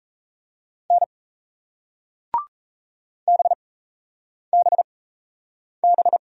Morse code, heard as a keyed tone, about 38 wpm, spaced as for 8 wpm, 700 Hz. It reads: N BBB